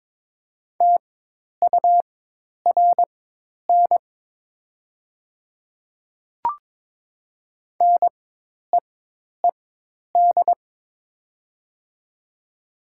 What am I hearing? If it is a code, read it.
Morse code, heard as a keyed tone, about 22 wpm, spaced as for 10 wpm, 700 Hz. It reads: TURN NEED
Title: 4x Character Spacing - Top 200 Words - 22wpm